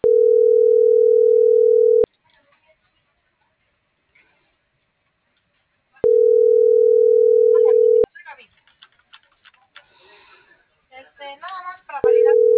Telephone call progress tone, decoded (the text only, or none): ringback tone